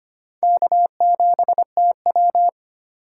Morse code, heard as a keyed tone, about 25 wpm, 700 Hz. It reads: K7TW
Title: USA Call Signs - 4 Characters - 2x Character Spacing - 25wpm